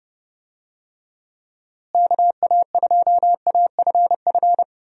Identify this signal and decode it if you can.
Morse code, heard as a keyed tone, about 32 wpm, 700 Hz. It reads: KA2AFF
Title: USA Call Signs - 6 Characters - Repeated 3x - 30wpm